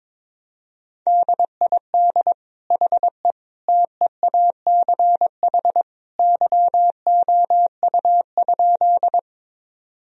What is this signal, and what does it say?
Morse code, heard as a keyed tone, about 22 wpm, 700 Hz. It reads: DID HE TEACH YOU?
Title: Sentences from Top 1000 Words - Review Part 2 - 22wpm